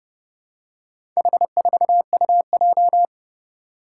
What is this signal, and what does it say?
Morse code, heard as a keyed tone, about 32 wpm, 700 Hz. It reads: H4UJ
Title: Four Challenging Letter-Number Combinations 30wpm